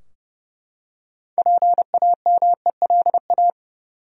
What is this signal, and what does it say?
Morse code, heard as a keyed tone, about 30 wpm, 700 Hz. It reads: PAMELA
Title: USA Common Names - 30wpm